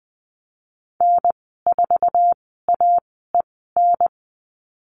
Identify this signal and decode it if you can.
Morse code, heard as a keyed tone, about 20 wpm, spaced as for 15 wpm, 700 Hz. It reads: N4AEN
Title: 048 Lesson 9g - Number 4 - Call Signs 20wpm